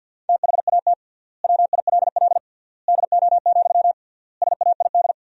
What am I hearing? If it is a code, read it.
Morse code, heard as a keyed tone, about 53 wpm, 700 Hz. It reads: THAT WILL DO, SAID
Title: 3x Word-Spacing - Sentences from Top 100 Words - 50wpm